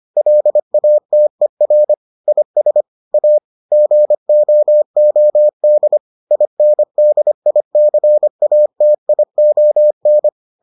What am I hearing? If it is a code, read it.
Morse code, heard as a keyed tone, about 25 wpm, 600 Hz. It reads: LATER IS A GOOD INDICATION